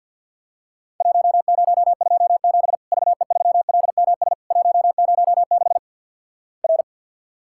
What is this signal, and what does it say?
Morse code, heard as a keyed tone, about 54 wpm, 700 Hz. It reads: 1027 VE3LMS 106 R
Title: Worked All Europe DX Contest - QTC Exchange - 50wpm